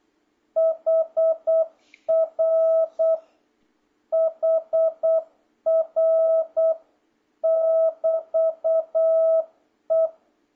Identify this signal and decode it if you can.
Morse code, heard as a keyed tone, about 8 wpm, 660 Hz. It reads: HR HR <BT> E